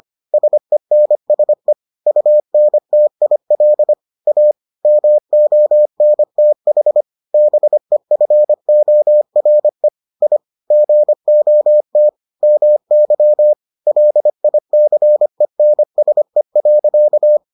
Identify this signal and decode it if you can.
Morse code, heard as a keyed tone, about 25 wpm, 600 Hz. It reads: SENSE UNTIL A MONTH BEFORE I GOT MY LICENSE.